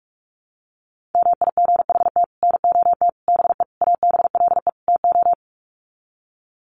Morse code, heard as a keyed tone, about 44 wpm, 700 Hz. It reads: MIGHT NOT BE ABLE TO
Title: Sets of 5 Words Encore2 45wpm